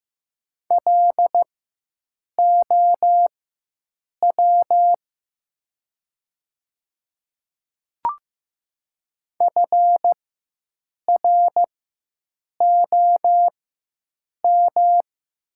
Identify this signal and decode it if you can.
Morse code, heard as a keyed tone, about 15 wpm, spaced as for 7 wpm, 700 Hz. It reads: LOW FROM
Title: 4x Character Spacing - Top 300 Words - 15wpm